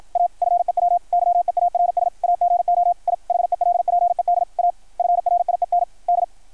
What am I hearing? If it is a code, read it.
Morse code, heard as a keyed tone, about 53 wpm, 690 Hz, with radio noise. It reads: A FEW YEARS AGO I HELPED A FRIEND
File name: archived / CRSnet